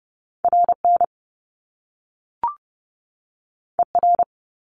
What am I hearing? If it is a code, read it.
Morse code, heard as a keyed tone, about 30 wpm, 700 Hz. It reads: RN ER
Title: Top Two Letter Combinations in Words 30wpm